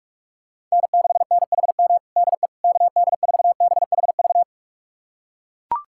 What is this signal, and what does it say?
Morse code, heard as a keyed tone, about 41 wpm, 700 Hz. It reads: N6NHM DE KD4BHV
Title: USA Callsign DE Callsign 45wpm